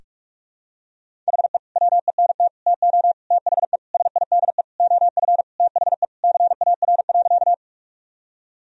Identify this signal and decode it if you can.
Morse code, heard as a keyed tone, about 46 wpm, 700 Hz. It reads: HE WENT TO THE SIDE OF THE CAR.
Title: Sentences from Top 1000 Words - Review Part 1 - 45wpm